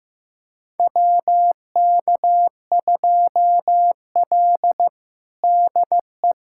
Morse code, heard as a keyed tone, about 15 wpm, 700 Hz. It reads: WK2L DE
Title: USA Callsign DE Callsign - 4 Characters - 15wpm